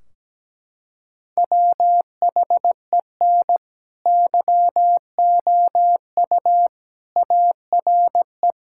Morse code, heard as a keyed tone, about 17 wpm, 700 Hz. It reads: WHEN YOU ARE
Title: Sentences from Top 1600 Words 17wpm